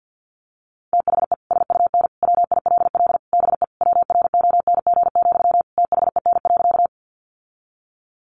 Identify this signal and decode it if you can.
Morse code, heard as a keyed tone, about 50 wpm, 700 Hz. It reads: THE SUN WILL BE WRONG, THEN.
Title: Sentences from Top 900 Words 50wpm